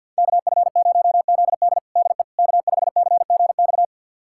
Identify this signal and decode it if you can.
Morse code, heard as a keyed tone, about 51 wpm, 700 Hz. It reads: KF0ZD DE K5CGX
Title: USA Callsign DE Callsign - 5 Characters - 50wpm